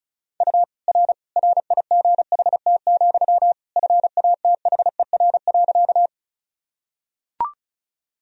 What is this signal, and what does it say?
Morse code, heard as a keyed tone, about 36 wpm, 700 Hz. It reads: U R RIGHT, FATHER.
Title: Sentences from Top 300 Words and Abbreviations 35wpm